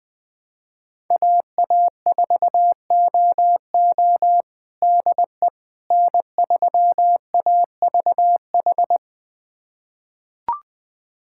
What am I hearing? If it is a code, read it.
Morse code, heard as a keyed tone, about 20 wpm, 700 Hz. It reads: AA4OO DE N3AVH
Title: USA Callsign DE Callsign - 5 Characters - 20wpm